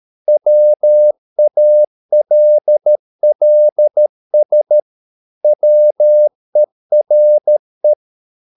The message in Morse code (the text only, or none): WALLS WERE